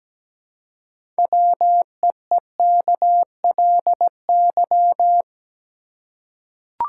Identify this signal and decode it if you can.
Morse code, heard as a keyed tone, about 17 wpm, 700 Hz. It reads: WEEKLY
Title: Six Letter Words - 17wpm